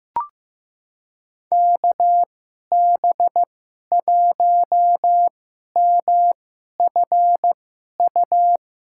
Morse code, heard as a keyed tone, about 15 wpm, spaced as for 11 wpm, 700 Hz. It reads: KB1MFU
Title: USA Call Signs - 6 Characters - 2x Character Spacing - 15wpm